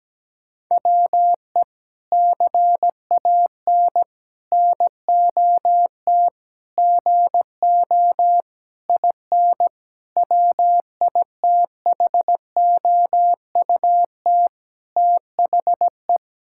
Morse code, heard as a keyed tone, about 17 wpm, 700 Hz. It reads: WE CAN NOT GO IN WITHOUT THE
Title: Sentences from Top 1000 Words - Review Part 6 - 17wpm